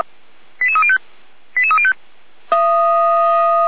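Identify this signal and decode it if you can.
Five-tone selective call (ZVEI), heard as quick runs of five tones, tones of about 70 ms. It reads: 88305 88305
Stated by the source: ZVEI five-tone selective call received on an SDR.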